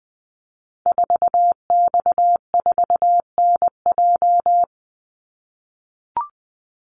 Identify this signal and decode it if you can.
Morse code, heard as a keyed tone, about 20 wpm, 700 Hz. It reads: 4X4NJ
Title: Call Signs - 20wpm